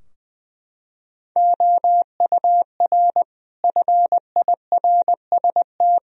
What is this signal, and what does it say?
Morse code, heard as a keyed tone, about 20 wpm, 700 Hz. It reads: OUR FIRST
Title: Sentences from Top 1000 Words - Review Part 3 - 20wpm